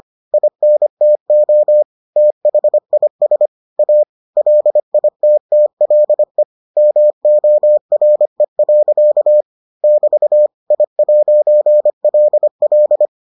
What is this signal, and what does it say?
Morse code, heard as a keyed tone, about 25 wpm, 600 Hz. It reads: INTO THIS A LITTLE MORE. <BT> I'LL